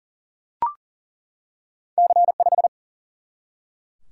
Morse code, heard as a keyed tone, about 40 wpm, 700 Hz. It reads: C5
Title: Single-Letter Single-Number 40wpm